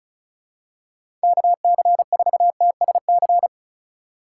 Morse code, heard as a keyed tone, about 36 wpm, 700 Hz. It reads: KC4TSC
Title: USA Call Signs - 6 Characters - 3x Character Spacing - 35wpm